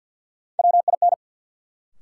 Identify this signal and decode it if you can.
Morse code, heard as a keyed tone, about 53 wpm, 700 Hz. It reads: WIN
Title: Sentences from Top 1500 Words - New Word Drill - 50wpm